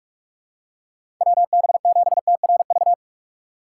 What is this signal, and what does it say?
Morse code, heard as a keyed tone, about 46 wpm, 700 Hz. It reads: WB7TRV
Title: USA Call Signs - 4x Character Spacing - 45wpm